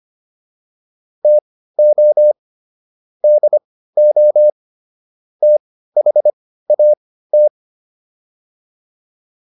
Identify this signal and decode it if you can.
Morse code, heard as a keyed tone, about 25 wpm, spaced as for 15 wpm, 600 Hz. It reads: TO DO THAT